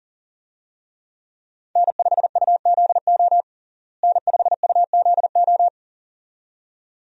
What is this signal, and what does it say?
Morse code, heard as a keyed tone, about 37 wpm, 700 Hz. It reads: N5UZO N5UZO
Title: USA Call Signs - 5 Characters - Repeated 2x - 40wpm